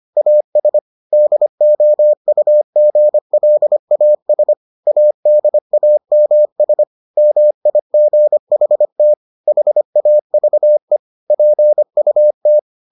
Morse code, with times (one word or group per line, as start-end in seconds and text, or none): 0.2-0.8 AS
1.1-4.5 DOUGLAS
4.9-6.8 ADAMS
7.2-9.1 MIGHT
9.5-11.0 HAVE
11.3-12.6 PUT